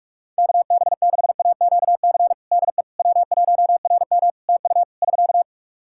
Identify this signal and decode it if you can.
Morse code, heard as a keyed tone, about 46 wpm, 700 Hz. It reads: KB6AQC DE W1RM TU <SK>
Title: Practice QSO - Exchange 4 - Ending - 45wpm